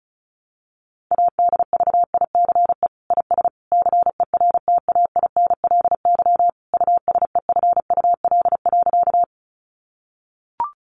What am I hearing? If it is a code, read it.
Morse code, heard as a keyed tone, about 34 wpm, 700 Hz. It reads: ADVICE IS CERTAINLY USEFUL.
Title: Sentences from Top 1100 Words 35wpm